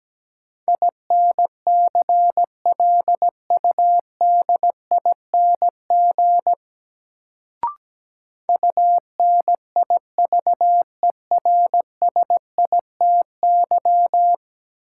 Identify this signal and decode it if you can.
Morse code, heard as a keyed tone, about 17 wpm, 700 Hz. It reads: INCLUDING UNIVERSITY
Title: Sentences from Top 1000 Words - Review Part 4 - New Word Drill - 17wpm